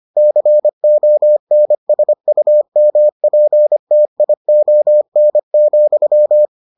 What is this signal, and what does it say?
Morse code, heard as a keyed tone, about 25 wpm, 600 Hz. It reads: CONSUMPTION,